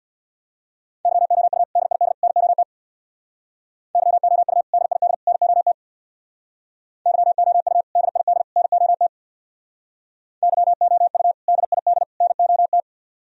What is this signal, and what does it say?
Morse code, heard as a keyed tone, about 63 wpm, 700 Hz. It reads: YOU DID NOT YOU DID NOT YOU DID NOT YOU DID NOT